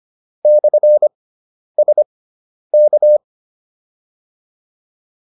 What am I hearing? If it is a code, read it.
Morse code, heard as a keyed tone, about 25 wpm, spaced as for 10 wpm, 600 Hz. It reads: /SK
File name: QSO306 10